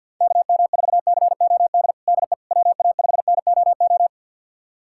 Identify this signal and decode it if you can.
Morse code, heard as a keyed tone, about 53 wpm, 700 Hz. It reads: KM4COD DE WA5NOO
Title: USA Callsign DE Callsign - 6 Characters - 50wpm